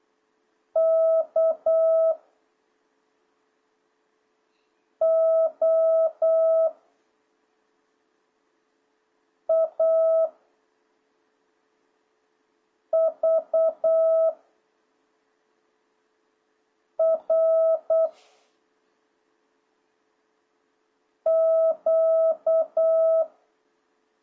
Morse code, heard as a keyed tone, about 8 wpm, 1300 Hz. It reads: K O A V R Q